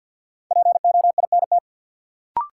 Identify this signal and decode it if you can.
Morse code, heard as a keyed tone, about 53 wpm, 700 Hz. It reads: POINT
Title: Sentences from Top 200 Words - New Word Drill - 50wpm